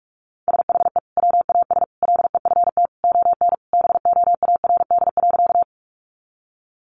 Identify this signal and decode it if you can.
Morse code, heard as a keyed tone, about 44 wpm, 700 Hz. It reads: SHE WAS LEFT ON BOARD.
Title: Sentences from Top 700 Words 45wpm